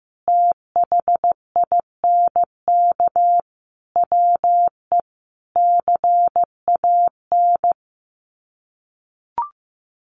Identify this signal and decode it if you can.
Morse code, heard as a keyed tone, about 15 wpm, 700 Hz. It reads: THINK WE CAN
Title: Sets of 5 Words Encore2 15wpm